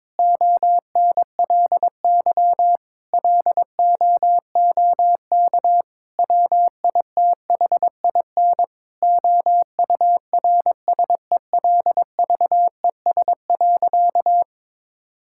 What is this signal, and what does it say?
Morse code, heard as a keyed tone, about 22 wpm, 700 Hz. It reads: ONLY LOOK WITHIN OURSELVES.
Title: Sentences from Top 1600 Words 22wpm